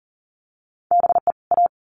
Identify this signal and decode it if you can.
Morse code, heard as a keyed tone, about 40 wpm, 700 Hz. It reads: BE A
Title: Sets of 2 Words 40wpm